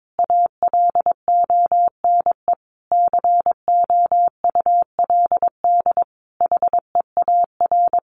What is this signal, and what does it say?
Morse code, heard as a keyed tone, about 22 wpm, 700 Hz. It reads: ALONE COULD HEAR
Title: Sentences from Top 1000 Words 22wpm